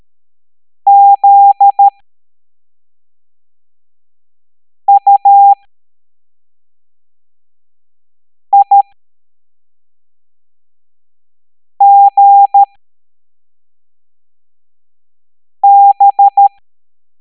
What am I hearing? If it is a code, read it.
Morse code, heard as a keyed tone, about 13 wpm, spaced as for 3 wpm, 800 Hz. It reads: ZUIGB